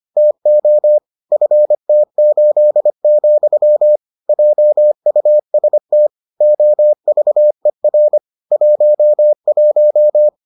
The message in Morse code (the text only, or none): TO FT8, JUST OVER 11